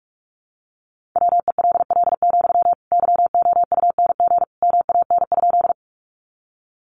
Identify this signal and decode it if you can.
Morse code, heard as a keyed tone, about 44 wpm, 700 Hz. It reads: WELL, YOUNG MAN?
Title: Sentences from Top 400 Words 45wpm